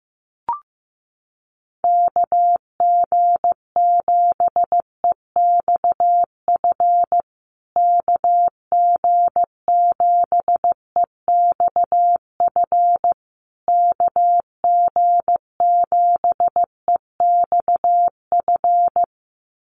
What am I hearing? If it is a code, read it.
Morse code, heard as a keyed tone, about 15 wpm, 700 Hz. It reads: KG7EXF KG7EXF KG7EXF
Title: USA Call Signs Repeated 3x 15wpm